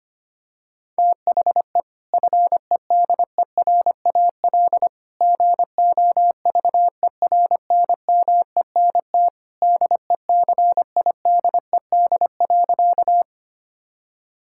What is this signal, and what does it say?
Morse code, heard as a keyed tone, about 25 wpm, 700 Hz. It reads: THE FEDERAL GOVERNMENT DECIDED.
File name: practice / 25wpm / Sentences with 3-Word Phrases - 25wpm